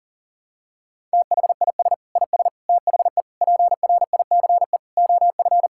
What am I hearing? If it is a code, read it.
Morse code, heard as a keyed tone, about 40 wpm, 700 Hz. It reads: THIS IS THE PRICE OF
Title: Sentences from Top 1000 Words - Review Part 5 - 40wpm